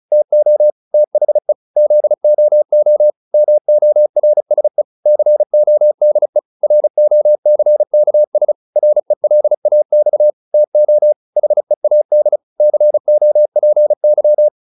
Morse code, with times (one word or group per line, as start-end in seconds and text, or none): 0.1-0.7 TO
0.9-1.5 THE
1.8-3.1 ZOO
3.3-4.8 MORSE
5.1-6.4 CODE
6.6-8.5 ROCKS
8.8-10.3 RELAX
10.5-11.1 TO
11.4-12.4 HEAD
12.6-14.5 COPY